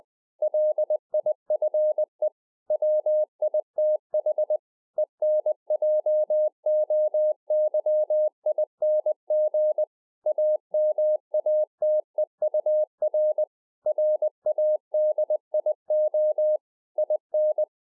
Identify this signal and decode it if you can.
Morse code, heard as a keyed tone, about 20 wpm, 610 Hz. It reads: LIFE WITH ENJOYING AMATEUR RADIO IN
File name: 20WPM